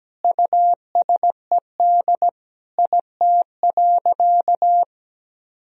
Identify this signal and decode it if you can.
Morse code, heard as a keyed tone, about 17 wpm, 700 Hz. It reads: USED IT.
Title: Sentences from Top 1000 Words - Review Part 2 - 17wpm